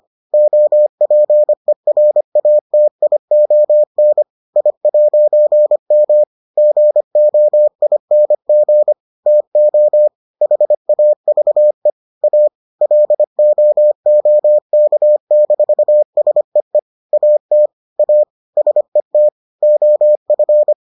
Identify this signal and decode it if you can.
Morse code, heard as a keyed tone, about 25 wpm, 600 Hz. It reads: OPERATION I'M GOING TO HAVE A LOOK-SEE AT A SET OF